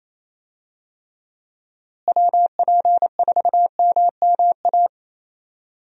Morse code, heard as a keyed tone, about 28 wpm, 700 Hz. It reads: WP4MMA
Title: USA Call Signs - 6 Characters - Repeated 2x - 28wpm